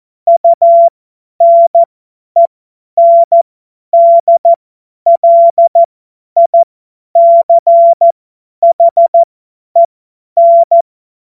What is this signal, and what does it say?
Morse code, heard as a keyed tone, about 14 wpm, spaced as for 10 wpm, 680 Hz. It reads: UNENDLICHEN
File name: Morsekurs / Witze / Witz 27 Februar 2024 50 F70